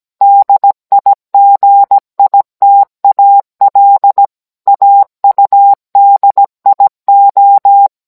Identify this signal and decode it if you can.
Morse code, heard as a keyed tone, about 17 wpm, 800 Hz. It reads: DIGITAL AUDIO